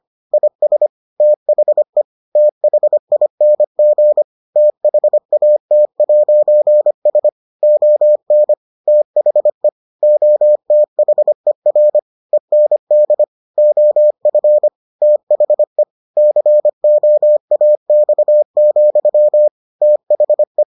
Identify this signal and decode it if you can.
Morse code, heard as a keyed tone, about 25 wpm, 600 Hz. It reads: IS THE THING THAT'S ON THE OTHER END OF THE COAX, THE